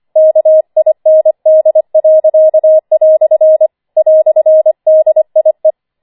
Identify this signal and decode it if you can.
Morse code, heard as a keyed tone, about 25 wpm, 630 Hz. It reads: KIND." "DIE
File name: 25wpm